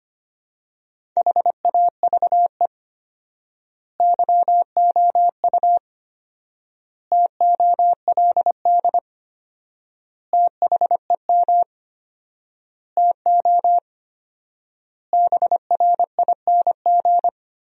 Morse code, heard as a keyed tone, about 25 wpm, 700 Hz. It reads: HAVE YOU TOLD THEM TO BRING
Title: 4x Word-Spacing - Sentences from Top 500 Words - 25wpm